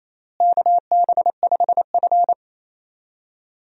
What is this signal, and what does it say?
Morse code, heard as a keyed tone, about 28 wpm, 700 Hz. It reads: KB5F
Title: USA Call Signs - 4 Characters - Repeated 3x - 28wpm